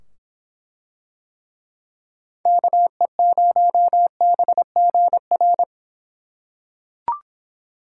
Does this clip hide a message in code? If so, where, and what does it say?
Yes: Morse code from 2.5 to 5.6 s, heard as a keyed tone, about 25 wpm, 700 Hz. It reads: KE0BGR